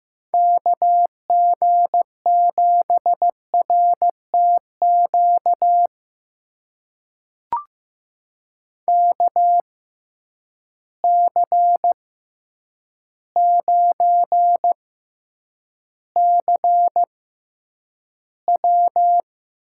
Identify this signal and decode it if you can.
Morse code, heard as a keyed tone, about 15 wpm, spaced as for 8 wpm, 700 Hz. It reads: KG7RTQ KC9CW
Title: USA Call Signs - 6x Character Spacing - 15wpm